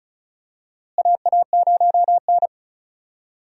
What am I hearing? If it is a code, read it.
Morse code, heard as a keyed tone, about 36 wpm, 700 Hz. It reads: AA0N
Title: USA Call Signs - 4 Characters - Repeated 2x - 35wpm